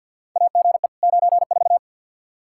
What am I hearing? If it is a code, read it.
Morse code, heard as a keyed tone, about 52 wpm, 700 Hz. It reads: AGE 94
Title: Practice QSO - Exchange 3 - More Info - 50wpm